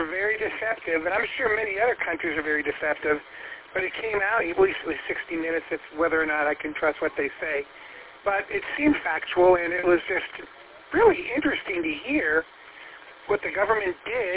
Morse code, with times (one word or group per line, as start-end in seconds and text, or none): none